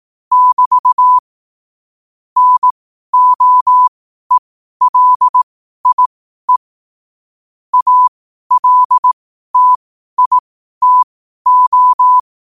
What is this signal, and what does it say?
Morse code, heard as a keyed tone, about 18 wpm, spaced as for 12 wpm, 1000 Hz. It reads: <BT> NOELIE ALTITO
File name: qotdfiles / 12wpm